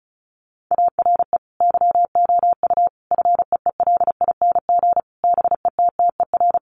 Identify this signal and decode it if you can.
Morse code, heard as a keyed tone, about 34 wpm, 700 Hz. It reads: ARE YOU FEELING BETTER
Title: Sentences from Top 1200 Words 35wpm